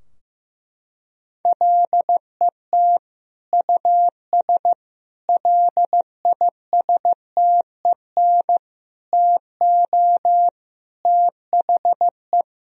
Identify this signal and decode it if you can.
Morse code, heard as a keyed tone, about 16 wpm, 700 Hz. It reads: LET US LISTEN TO THE
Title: Sentences from Top 1000 Words - Review Part 3 - 15wpm